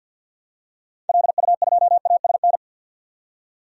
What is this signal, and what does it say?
Morse code, heard as a keyed tone, about 53 wpm, 700 Hz. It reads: LU2ASN
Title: DXCC-100 Call Signs - 6 Characters - 50wpm